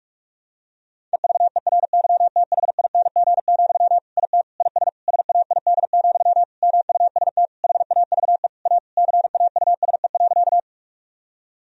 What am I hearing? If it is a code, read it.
Morse code, heard as a keyed tone, about 46 wpm, 700 Hz. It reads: EVERYTHING, IT IS SAID, MUST HAVE A CAUSE.
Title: Sentences from Top 1000 Words - Review Part 1 - 45wpm